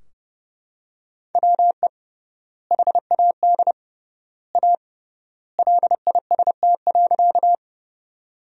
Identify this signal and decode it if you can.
Morse code, heard as a keyed tone, about 30 wpm, 700 Hz. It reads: WE HAD A LIST.
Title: 3x Word-Spacing - Sentences from Top 400 Words - 30wpm